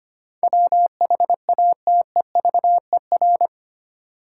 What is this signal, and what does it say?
Morse code, heard as a keyed tone, about 25 wpm, 700 Hz. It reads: WHATEVER
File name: 8 Letter Words - First 3 Letters Given - 25wpm